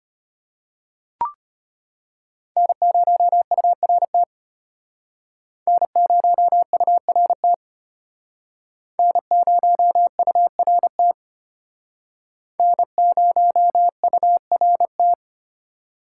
Morse code, heard as a keyed tone, about 32 wpm, 700 Hz. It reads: N0URT N0URT N0URT N0URT